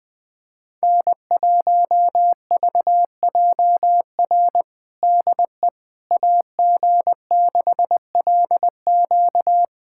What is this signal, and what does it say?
Morse code, heard as a keyed tone, about 20 wpm, 700 Hz. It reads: N1VJR DE AG6LQ
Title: USA Callsign DE Callsign - 5 Characters - 20wpm